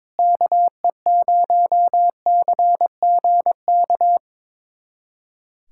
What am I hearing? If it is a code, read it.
Morse code, heard as a keyed tone, about 22 wpm, 700 Hz. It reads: KE0CGK
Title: USA Callsign DE Callsign 22wpm